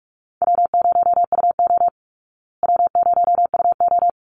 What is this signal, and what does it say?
Morse code, heard as a keyed tone, about 44 wpm, 700 Hz. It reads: W0UO W0UO
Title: Easy Call Signs Repeated 2x - 45wpm